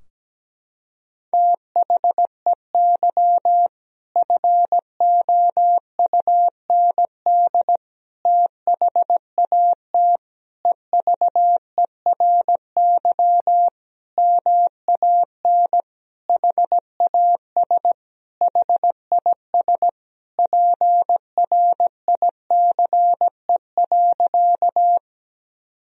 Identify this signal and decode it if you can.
Morse code, heard as a keyed tone, about 17 wpm, 700 Hz. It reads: THEY FOUND THAT EVERY MAN HAS HIS PRICE.
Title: Sentences from Top 1000 Words - Review Part 4 - 17wpm